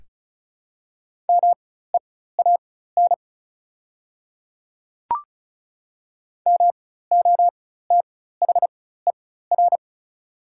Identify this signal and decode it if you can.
Morse code, heard as a keyed tone, about 36 wpm, spaced as for 16 wpm, 700 Hz. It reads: MEAN MOTHER
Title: 4x Character Spacing - Top 200 Words - 35wpm